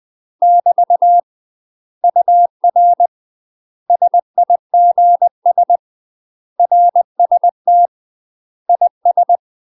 Morse code, heard as a keyed tone, about 20 wpm, 700 Hz. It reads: <BT> UR SIGS RST IS
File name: UK MASTER QSOs TRAINING / 20WPM / UK MASTER QSOs 0058